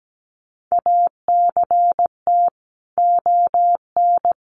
Morse code, heard as a keyed tone, about 17 wpm, 700 Hz. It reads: ACT ON
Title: Sentences from Top 1400 Words 17wpm